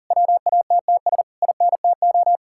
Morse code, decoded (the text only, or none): WATTS INTO